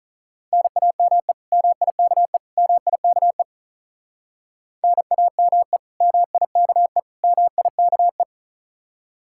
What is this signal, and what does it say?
Morse code, heard as a keyed tone, about 45 wpm, 700 Hz. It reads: NAME MIKE MIKE NAME MIKE MIKE